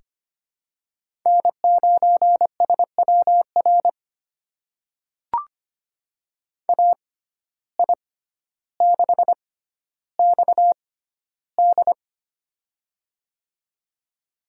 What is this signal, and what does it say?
Morse code, heard as a keyed tone, about 25 wpm, spaced as for 13 wpm, 700 Hz. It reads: N9SWR AI6XD
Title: USA Call Signs - 5 Characters - 6x Character Spacing - 25wpm